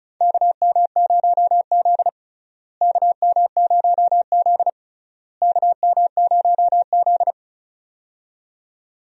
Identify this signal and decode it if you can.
Morse code, heard as a keyed tone, about 34 wpm, 700 Hz. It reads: KM0Z KM0Z KM0Z
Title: USA Call Signs - 4 Characters - Repeated 3x - 35wpm